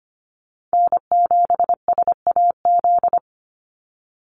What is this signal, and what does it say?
Morse code, heard as a keyed tone, about 25 wpm, 700 Hz. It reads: N7SAZ
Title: USA Call Signs 25wpm